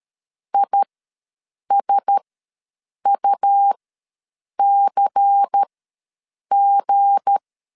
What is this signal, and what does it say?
Morse code, heard as a keyed tone, about 13 wpm, spaced as for 7 wpm, 800 Hz. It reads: ISUCG
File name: LEZIONI / 8 YJ45 / 8 Y ws 12wpm 3409